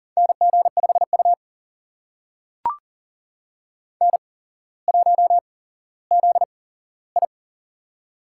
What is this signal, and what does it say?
Morse code, heard as a keyed tone, about 40 wpm, spaced as for 17 wpm, 700 Hz. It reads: NG5U N1ZI
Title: USA Call Signs - 4 Characters - 8x Character Spacing - 40wpm